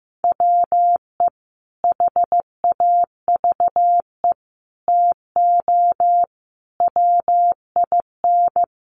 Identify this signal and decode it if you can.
Morse code, heard as a keyed tone, about 15 wpm, 700 Hz. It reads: WE HAVE TO WIN
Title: Sentences from Top 1500 Words 15wpm